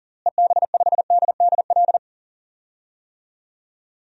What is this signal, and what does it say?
Morse code, heard as a keyed tone, about 40 wpm, 700 Hz. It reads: EB5DDL
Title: DXCC-100 Call Signs - 6 Characters - Repeated 2x - 40wpm